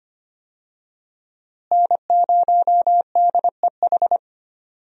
Morse code, heard as a keyed tone, about 25 wpm, 700 Hz. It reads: N0DEH
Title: USA Call Signs - 5 Characters - Repeated 3x - 25wpm